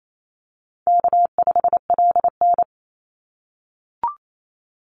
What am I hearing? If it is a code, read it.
Morse code, heard as a keyed tone, about 28 wpm, 700 Hz. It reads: K5LN
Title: Call Signs - 28wpm